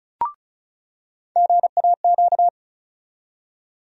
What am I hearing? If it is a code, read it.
Morse code, heard as a keyed tone, about 34 wpm, 700 Hz. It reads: GAQ
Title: USA Call Sign Postfixes - 3 Letters 35wpm